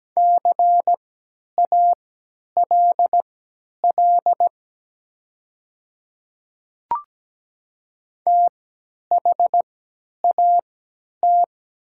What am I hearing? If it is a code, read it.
Morse code, heard as a keyed tone, about 17 wpm, spaced as for 10 wpm, 700 Hz. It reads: CALL THAT
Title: 3x Character Spacing - Top 200 Words - 17wpm